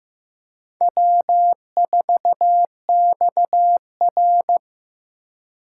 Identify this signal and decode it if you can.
Morse code, heard as a keyed tone, about 15 wpm, 700 Hz. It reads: W4XR
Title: USA Call Signs - 2x Character Spacing - 15wpm